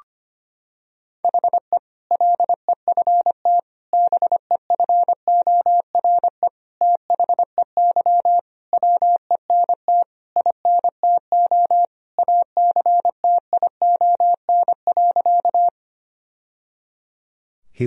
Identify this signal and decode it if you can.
Morse code, heard as a keyed tone, about 25 wpm, 700 Hz. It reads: HE LEFT BEFORE THEY WENT INTO ACTION.
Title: Sentences from Top 1000 Words - Review Part 4 - 25wpm